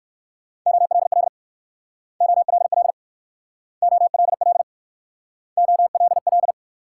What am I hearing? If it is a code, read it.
Morse code, heard as a keyed tone, about 54 wpm, 690 Hz. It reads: OLL OLL OLL OLL
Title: Speed-Racing Multi-letter Phonemes - 45wpm